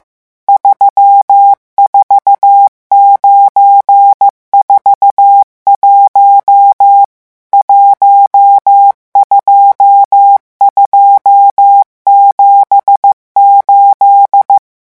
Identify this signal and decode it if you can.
Morse code, heard as a keyed tone, about 15 wpm, 800 Hz. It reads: 34941 12278
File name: morse mp3 / revision11 chiffres vitesse 14